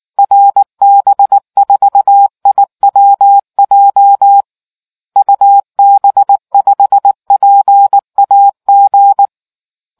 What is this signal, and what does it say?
Morse code, heard as a keyed tone, about 19 wpm, 800 Hz. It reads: RB4IWJ UB5PAG